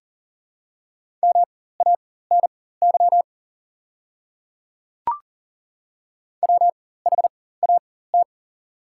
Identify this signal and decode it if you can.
Morse code, heard as a keyed tone, about 40 wpm, spaced as for 19 wpm, 700 Hz. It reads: MANY WHAT
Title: 4x Character Spacing - Top 100 Words - 40wpm